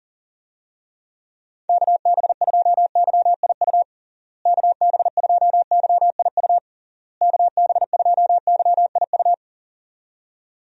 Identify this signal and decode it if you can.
Morse code, heard as a keyed tone, about 40 wpm, 700 Hz. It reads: KB2YIU KB2YIU KB2YIU
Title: USA Call Signs - 6 Characters - Repeated 3x - 40wpm